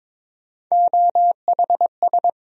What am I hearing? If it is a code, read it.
Morse code, heard as a keyed tone, about 22 wpm, 700 Hz. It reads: OHS
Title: USA Call Sign Postfixes - 3 Letters 22wpm